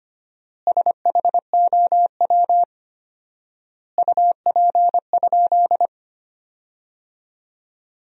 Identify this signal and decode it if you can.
Morse code, heard as a keyed tone, about 25 wpm, 700 Hz. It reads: SHOW UP?
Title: 4x Word-Spacing - Sentences from Top 200 Words - 25wpm